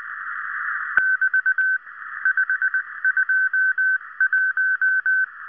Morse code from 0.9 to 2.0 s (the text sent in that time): <BT>